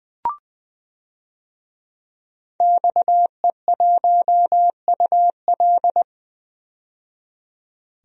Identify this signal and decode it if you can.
Morse code, heard as a keyed tone, about 20 wpm, 700 Hz. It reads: XE1UL
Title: DXCC-100 Call Signs - 5 Characters - 20wpm